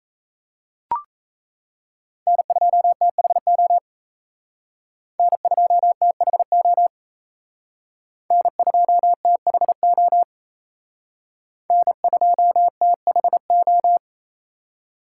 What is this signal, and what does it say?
Morse code, heard as a keyed tone, about 34 wpm, 700 Hz. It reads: N2THO N2THO N2THO N2THO
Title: Speed Racing - USA Call Signs 28wpm